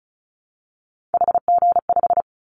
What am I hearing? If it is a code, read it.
Morse code, heard as a keyed tone, about 34 wpm, 700 Hz. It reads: HG5
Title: Callsign Prefixes for Top 25 DXCC - Repeated 3x - 35wpm